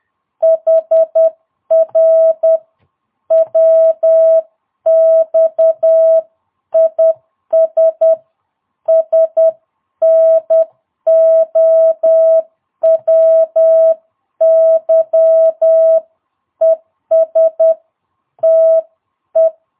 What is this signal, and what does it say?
Morse code, heard as a keyed tone, about 10 wpm, 650 Hz. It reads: HR WX IS SNOWY ES TE
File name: jh8cbh / 2022cw